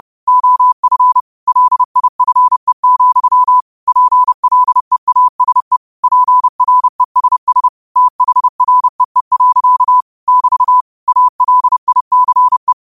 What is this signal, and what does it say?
Morse code, heard as a keyed tone, about 30 wpm, 1000 Hz. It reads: OF LIFE, PLEASE PRESS THREE. <BT> ALICE
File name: qotdfiles / 30wpm